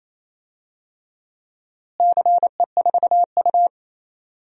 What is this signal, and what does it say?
Morse code, heard as a keyed tone, about 27 wpm, 700 Hz. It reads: CE4U
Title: DXCC-100 Call Signs - 4 Characters - Repeated 2x - 28wpm